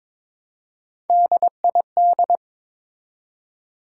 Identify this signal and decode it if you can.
Morse code, heard as a keyed tone, about 22 wpm, 700 Hz. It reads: DID
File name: Build A Phrase - 2-Word Phrases - 22wpm